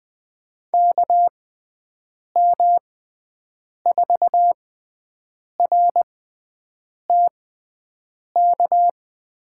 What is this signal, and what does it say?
Morse code, heard as a keyed tone, about 20 wpm, spaced as for 7 wpm, 700 Hz. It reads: KM4RTK
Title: USA Call Signs - 6 Characters - 6x Character Spacing - 20wpm